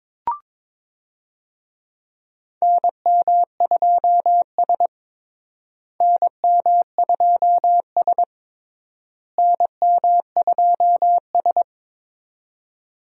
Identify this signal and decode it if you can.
Morse code, heard as a keyed tone, about 22 wpm, 700 Hz. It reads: NM2S NM2S NM2S
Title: USA Call Signs - 4 Characters - Repeated 3x - 22wpm